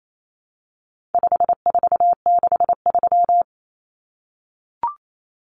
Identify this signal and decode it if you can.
Morse code, heard as a keyed tone, about 28 wpm, 700 Hz. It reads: 5463
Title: Numbers - 4-Digit 28wpm